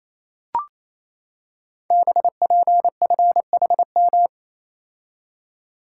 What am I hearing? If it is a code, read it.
Morse code, heard as a keyed tone, about 27 wpm, 700 Hz. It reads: BPFHM
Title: Code Groups - 5 Letters - 28wpm